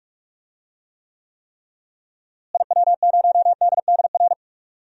Morse code, heard as a keyed tone, about 44 wpm, 700 Hz. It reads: IW0DDR
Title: DXCC-100 Call Signs - 6 Characters - Repeated 3x - 45wpm